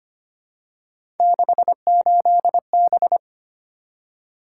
Morse code, heard as a keyed tone, about 25 wpm, 700 Hz. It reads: 68B
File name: Three Letter-Number Combinations - 25wpm